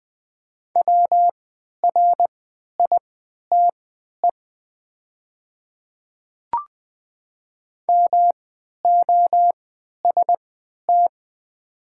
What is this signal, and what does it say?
Morse code, heard as a keyed tone, about 20 wpm, spaced as for 11 wpm, 700 Hz. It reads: WRITE MOST
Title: 3x Character Spacing - Top 100 Words - 20wpm